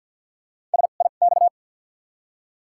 Morse code, heard as a keyed tone, about 50 wpm, spaced as for 36 wpm, 700 Hz. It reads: SIX
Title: 2x Character Spacing - Top 500 Words - 50wpm